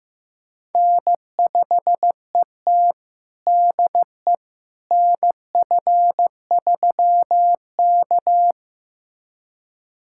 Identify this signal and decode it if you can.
Morse code, heard as a keyed tone, about 15 wpm, 700 Hz. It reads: N5ET DE NF3K
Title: USA Callsign DE Callsign - 4 Characters - 15wpm